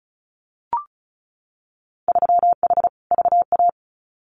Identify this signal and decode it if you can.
Morse code, heard as a keyed tone, about 36 wpm, 700 Hz. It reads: 3H VA
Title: Winter Field Day Exchanges 35wpm